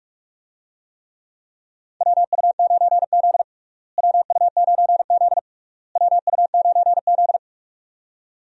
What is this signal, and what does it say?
Morse code, heard as a keyed tone, about 42 wpm, 700 Hz. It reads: WU9Z WU9Z WU9Z
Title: USA Call Signs - 4 Characters - Repeated 3x - 45wpm